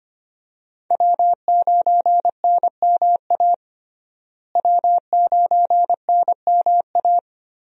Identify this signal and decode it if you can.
Morse code, heard as a keyed tone, about 25 wpm, 700 Hz. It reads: W9NMA W9NMA
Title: USA Call Signs - 5 Characters - Repeated 2x - 25wpm